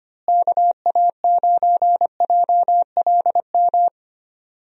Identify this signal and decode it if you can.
Morse code, heard as a keyed tone, about 25 wpm, 700 Hz. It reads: KA9JLM
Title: USA Call Signs - 6 Characters - Repeated 2x - 25wpm